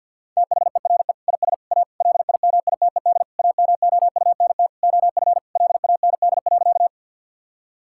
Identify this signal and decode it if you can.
Morse code, heard as a keyed tone, about 53 wpm, 700 Hz. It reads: THERE IS A LIMITED AMOUNT OF LAND.
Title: Sentences from Top 1000 Words - Review Part 6 - 50wpm